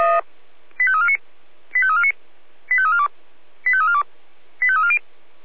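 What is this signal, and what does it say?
Five-tone selective call (ZVEI), heard as quick runs of five tones, tones of about 70 ms. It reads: 85349 85349 85342 85342 85340
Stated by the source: Five-tone ZVEI selective call 85275, received by SDR.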